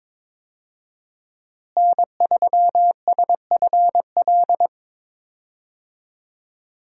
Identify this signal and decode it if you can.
Morse code, heard as a keyed tone, about 22 wpm, 700 Hz. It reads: N3SFL
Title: USA Call Signs - 5 Characters - Repeated 3x - 22wpm